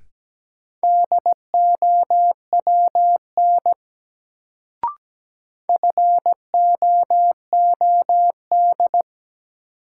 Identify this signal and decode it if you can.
Morse code, heard as a keyed tone, about 17 wpm, 700 Hz. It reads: DOWN FOOD
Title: Sentences from Top 300 Words - New Word Drill - 17wpm